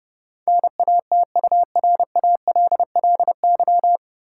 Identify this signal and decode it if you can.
Morse code, heard as a keyed tone, about 30 wpm, 700 Hz. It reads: NATURALLY
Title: Sentences from Top 1300 Words - New Word Drill Alt - 30wpm